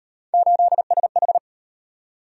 Morse code, heard as a keyed tone, about 34 wpm, 700 Hz. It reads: 8SH